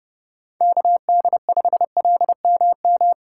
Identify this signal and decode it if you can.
Morse code, heard as a keyed tone, about 31 wpm, 700 Hz. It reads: KD5LMM